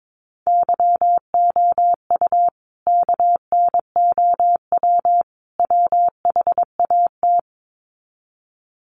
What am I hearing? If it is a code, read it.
Morse code, heard as a keyed tone, about 22 wpm, 700 Hz. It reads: YOU KNOW WHAT
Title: Sets of 3 Words 22wpm